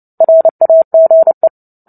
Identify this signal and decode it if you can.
Morse code, heard as a keyed tone, about 30 wpm, 650 Hz. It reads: RAGE
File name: Book Two / 03-Chapter 3 - 29 WPM